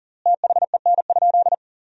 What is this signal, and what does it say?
Morse code, heard as a keyed tone, about 40 wpm, 700 Hz. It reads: THEN?
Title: 3x Word-Spacing - Sentences from Top 100 Words - 40wpm